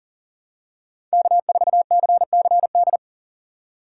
Morse code, heard as a keyed tone, about 40 wpm, 700 Hz. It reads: K4CCD
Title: USA Call Signs - 5 Characters - 6x Character Spacing - 40wpm